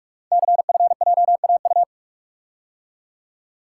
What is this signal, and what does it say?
Morse code, heard as a keyed tone, about 44 wpm, 700 Hz. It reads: CFJAU
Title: Code Groups - 5 Letters - 45wpm